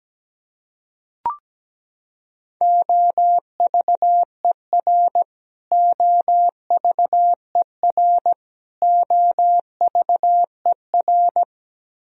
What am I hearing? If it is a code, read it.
Morse code, heard as a keyed tone, about 17 wpm, 700 Hz. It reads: OVER OVER OVER
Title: Top 100 Words Repeated 17wpm